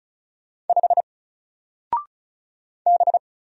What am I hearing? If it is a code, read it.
Morse code, heard as a keyed tone, about 36 wpm, 700 Hz. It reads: 5 B
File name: Single-Letter-Number-35wpm